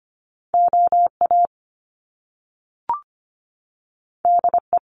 Morse code, heard as a keyed tone, about 25 wpm, 700 Hz. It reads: OA DE
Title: Top Two Letter Combinations in Words 25wpm